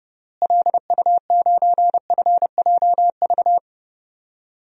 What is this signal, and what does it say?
Morse code, heard as a keyed tone, about 30 wpm, 700 Hz. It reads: LU9FJV
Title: DXCC-100 Call Signs - 6 Characters - Repeated 3x - 30wpm